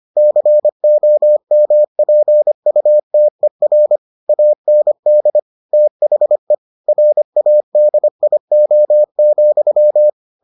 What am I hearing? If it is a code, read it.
Morse code, heard as a keyed tone, about 25 wpm, 600 Hz. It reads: COMPUTER AND THE RADIO,